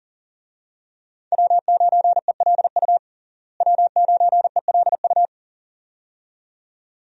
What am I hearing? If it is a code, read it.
Morse code, heard as a keyed tone, about 40 wpm, 700 Hz. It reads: W9ELU W9ELU
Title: USA Call Signs - 5 Characters - Repeated 2x - 40wpm